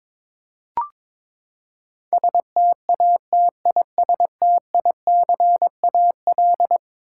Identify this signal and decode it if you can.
Morse code, heard as a keyed tone, about 22 wpm, 700 Hz. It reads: STATISTICAL
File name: practice / 22wpm / Eleven Letter Words - 22wpm